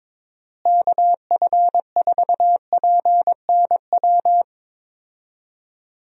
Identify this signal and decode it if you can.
Morse code, heard as a keyed tone, about 22 wpm, 700 Hz. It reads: KF4PNW
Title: USA Call Signs - 6 Characters - Repeated 2x - 22wpm